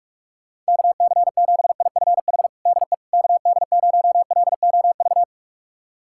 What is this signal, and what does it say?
Morse code, heard as a keyed tone, about 46 wpm, 700 Hz. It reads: KC7IFH DE KD0LOV
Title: USA Callsign DE Callsign 45wpm